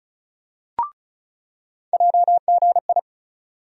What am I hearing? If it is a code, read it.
Morse code, heard as a keyed tone, about 36 wpm, 700 Hz. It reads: JGI
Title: USA Call Sign Postfixes - 3 Letters 35wpm